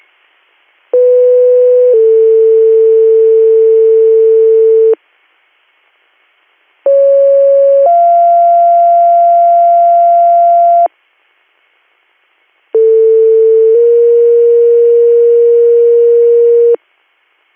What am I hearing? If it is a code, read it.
Two-tone page, heard as a one-second tone, then a three-second tone, about 500 then 450 Hz, about 550 then 690 Hz, about 450 then 470 Hz. It is heard from 0.9 to 16.8 s.